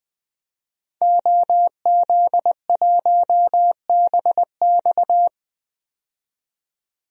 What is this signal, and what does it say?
Morse code, heard as a keyed tone, about 20 wpm, 700 Hz. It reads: OZ1BX
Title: DXCC-100 Call Signs - 5 Characters - 20wpm